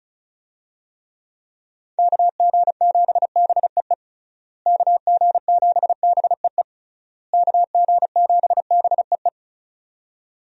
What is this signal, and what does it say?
Morse code, heard as a keyed tone, about 34 wpm, 700 Hz. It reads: KG7BEE KG7BEE KG7BEE
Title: USA Call Signs - 6 Characters - Repeated 3x - 35wpm